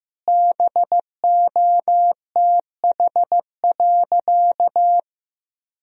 Morse code, heard as a keyed tone, about 15 wpm, 700 Hz. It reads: BOTH.
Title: Sentences from Top 1000 Words - Review Part 3 - 15wpm